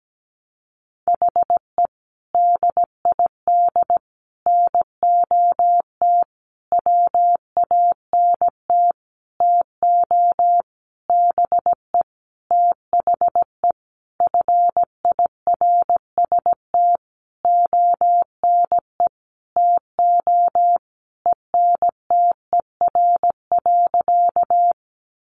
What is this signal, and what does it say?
Morse code, heard as a keyed tone, about 17 wpm, 700 Hz. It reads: HE DID NOT WANT TO BE THE FIRST ONE TO ENTER.
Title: Sentences from Top 1300 Words 17wpm